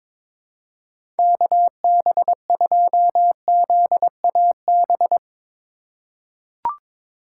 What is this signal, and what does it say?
Morse code, heard as a keyed tone, about 22 wpm, 700 Hz. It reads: KB2ZAB
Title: USA Call Signs - 6 Characters - 3x Character Spacing - 22wpm